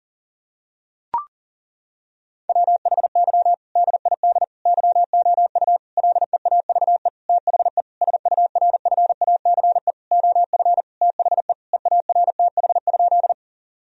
Morse code, heard as a keyed tone, about 40 wpm, 700 Hz. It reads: WHY DID YOU LEAVE THE SURFACE OF THE EARTH?
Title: Sentences from Top 1000 Words - Review Part 3 - 40wpm